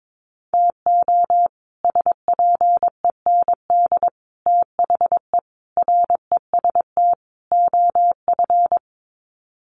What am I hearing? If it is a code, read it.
Morse code, heard as a keyed tone, about 22 wpm, 700 Hz. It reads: TO SPEND THE REST OF
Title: Sets of 5 Words Encore2 22wpm